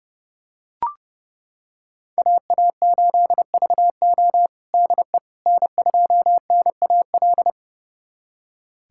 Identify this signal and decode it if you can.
Morse code, heard as a keyed tone, about 32 wpm, 700 Hz. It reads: AA8VO DE N2NAL
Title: USA Callsign DE Callsign - 5 Characters - 30wpm